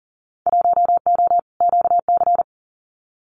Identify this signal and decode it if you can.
Morse code, heard as a keyed tone, about 39 wpm, 700 Hz. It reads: 1O QC
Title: Winter Field Day Exchanges 40wpm